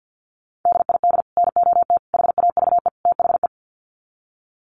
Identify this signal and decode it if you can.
Morse code, heard as a keyed tone, about 50 wpm, 700 Hz. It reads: DID NOT HAVE THE
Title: Sets of 4 Words 50wpm